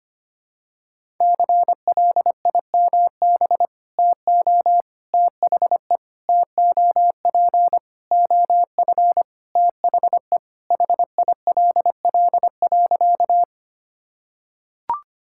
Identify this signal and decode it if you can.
Morse code, heard as a keyed tone, about 25 wpm, 700 Hz. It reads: CLIMB TO THE TOP OF THE HILL.
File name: practice / 25wpm / Sentences with 4-Word Phrases - 25wpm